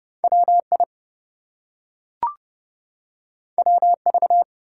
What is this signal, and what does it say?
Morse code, heard as a keyed tone, about 30 wpm, 700 Hz. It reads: WI WV
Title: US States and Canadian Provinces 30wpm